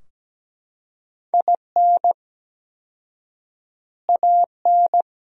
Morse code, heard as a keyed tone, about 17 wpm, 700 Hz. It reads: IN AN
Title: 4x Word-Spacing - Sentences from Top 400 Words - 17wpm